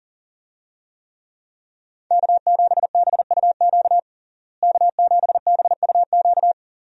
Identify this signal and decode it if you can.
Morse code, heard as a keyed tone, about 40 wpm, 700 Hz. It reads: K7BUQ K7BUQ
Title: USA Call Signs - 5 Characters - Repeated 2x - 40wpm